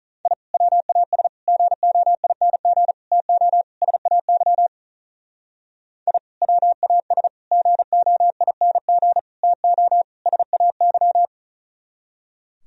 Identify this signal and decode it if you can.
Morse code, heard as a keyed tone, about 43 wpm, 700 Hz. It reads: I WAS GOING TO SAY I WAS GOING TO SAY
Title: Speed-Racing - Sets of 5 Words 35wpm